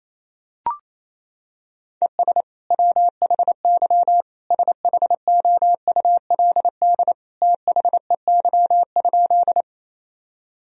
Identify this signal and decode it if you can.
Morse code, heard as a keyed tone, about 28 wpm, 700 Hz. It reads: ES WHY SHOULD THEY?
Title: Sentences from Top 300 Words and Abbreviations 28wpm